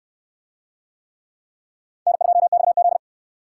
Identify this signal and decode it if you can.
Morse code, heard as a keyed tone, about 68 wpm, 700 Hz. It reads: N2XZ